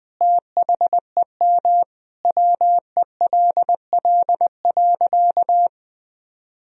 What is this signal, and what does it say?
Morse code, heard as a keyed tone, about 20 wpm, 700 Hz. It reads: THEM WELL.
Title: Sentences from Top 1000 Words - Review Part 1 - 20wpm